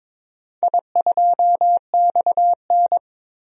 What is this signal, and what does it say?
Morse code, heard as a keyed tone, about 23 wpm, 700 Hz. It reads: I2XN
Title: DXCC-100 Call Signs - 4 Characters - 22wpm